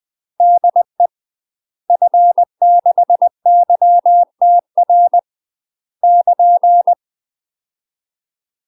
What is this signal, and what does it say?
Morse code, heard as a keyed tone, about 20 wpm, 700 Hz. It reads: DE F6YTR <KN>
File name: UK MASTER QSOs TRAINING/20WPM